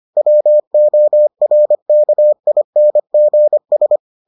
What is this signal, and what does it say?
Morse code, heard as a keyed tone, about 25 wpm, 600 Hz. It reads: WORKINGS